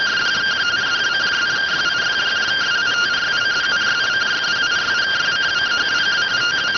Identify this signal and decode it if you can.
Two-tone teleprinter signal, about 170 Hz shift.